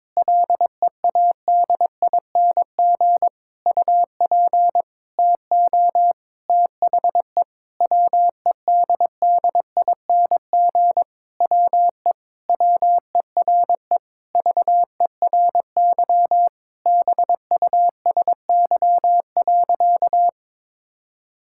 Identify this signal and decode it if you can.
Morse code, heard as a keyed tone, about 22 wpm, 700 Hz. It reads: LEADING UP TO THE WEDDING WE WERE VERY BUSY.
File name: Sentences with 7-Word Phrases - 22wpm